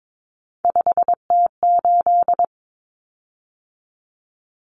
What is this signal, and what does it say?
Morse code, heard as a keyed tone, about 22 wpm, 700 Hz. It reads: 5T8
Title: Common Cut Numbers - 3-Digit 22wpm